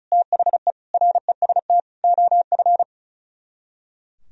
Morse code, heard as a keyed tone, about 45 wpm, 700 Hz. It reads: THE REST OF